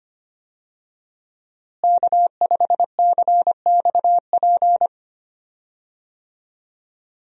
Morse code, heard as a keyed tone, about 25 wpm, 700 Hz. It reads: K5CXP
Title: USA Call Signs - 5 Characters - Repeated 3x - 25wpm